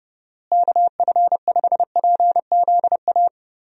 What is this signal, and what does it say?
Morse code, heard as a keyed tone, about 31 wpm, 700 Hz. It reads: KF5PZA
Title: USA Call Signs - 6 Characters - Repeated 3x - 30wpm